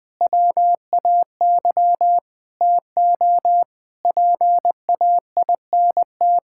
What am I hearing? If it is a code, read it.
Morse code, heard as a keyed tone, about 20 wpm, 700 Hz. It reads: WAY TO PAINT
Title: Sentences with 8-Word Phrases - Spoken Beginning - 20wpm